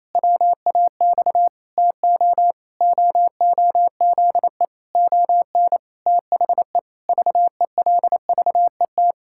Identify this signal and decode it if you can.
Morse code, heard as a keyed tone, about 28 wpm, 700 Hz. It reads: WAX TO OOZE ON THE VELVET
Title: Pangram Challenge v2 - 28wpm